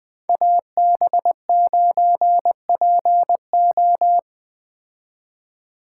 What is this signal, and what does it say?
Morse code, heard as a keyed tone, about 21 wpm, 700 Hz. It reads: AB9PO